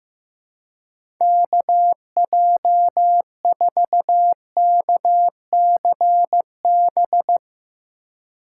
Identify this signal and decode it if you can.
Morse code, heard as a keyed tone, about 15 wpm, 700 Hz. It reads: KJ4KCB
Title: USA Call Signs - 6 Characters - 4x Character Spacing - 15wpm